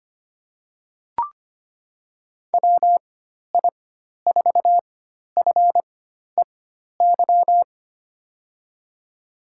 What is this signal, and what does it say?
Morse code, heard as a keyed tone, about 25 wpm, spaced as for 12 wpm, 700 Hz. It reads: WI4FEY
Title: USA Call Signs - 4x Character Spacing - 25wpm